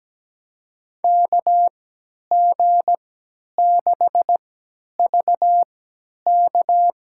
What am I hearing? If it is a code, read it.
Morse code, heard as a keyed tone, about 17 wpm, spaced as for 10 wpm, 700 Hz. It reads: KG6VK
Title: USA Call Signs - 3x Character Spacing - 17wpm